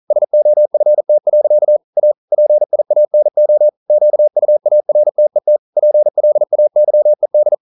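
Morse code, heard as a keyed tone, about 42 wpm, 600 Hz. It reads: SOFT. A PIANO QUARTET PLAYED
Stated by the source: Morse code practice at 41 wpm, from CW med Gnister.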